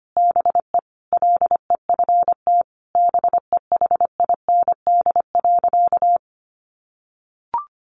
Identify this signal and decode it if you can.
Morse code, heard as a keyed tone, about 25 wpm, 700 Hz. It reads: BE LEFT BEHIND.
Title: A Sound Of Thunder 25wpm